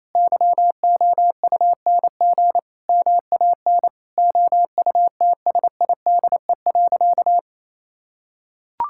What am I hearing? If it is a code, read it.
Morse code, heard as a keyed tone, about 28 wpm, 700 Hz. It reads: YOUNG MAN OUTSIDE.
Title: Sentences from Top 1000 Words - Review Part 6 - 28wpm